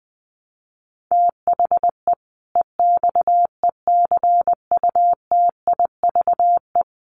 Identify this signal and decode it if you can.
Morse code, heard as a keyed tone, about 20 wpm, 700 Hz. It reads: THE EXECUTIVE
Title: Sets of 5 Words Encore 20wpm